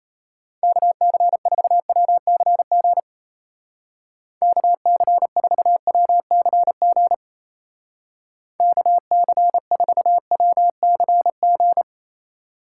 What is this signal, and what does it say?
Morse code, heard as a keyed tone, about 35 wpm, 700 Hz. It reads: KC4WCG KC4WCG KC4WCG